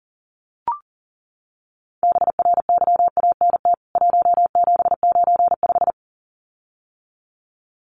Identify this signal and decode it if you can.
Morse code, heard as a keyed tone, about 40 wpm, 700 Hz. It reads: BRYANT 1795
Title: CWT Exchange 40wpm